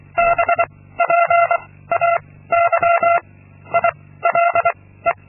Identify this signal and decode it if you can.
Morse code, heard as a keyed tone, about 23 wpm, 680 Hz. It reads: B P A Y I L E
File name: Linksmakalnis-2